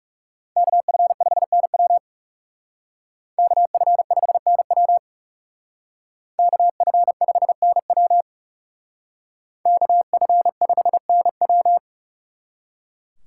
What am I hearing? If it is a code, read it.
Morse code, heard as a keyed tone, about 36 wpm, 700 Hz. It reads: KF5NW KF5NW KF5NW KF5NW
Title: Speed Racing - USA 2x2 Call Signs 30wpm